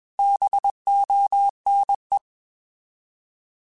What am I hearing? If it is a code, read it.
Morse code, heard as a keyed tone, about 21 wpm, 800 Hz. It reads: BONE
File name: Alphabetical / Alphabetical mp3s / Bs @20